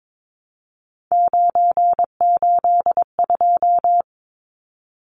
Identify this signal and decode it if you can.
Morse code, heard as a keyed tone, about 22 wpm, 700 Hz. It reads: 982
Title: Numbers - 3-Digit 22wpm